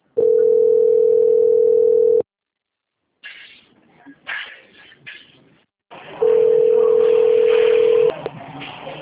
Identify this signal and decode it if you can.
Ringback tone (North America), 440 + 480 Hz.
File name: out- - - - - .89442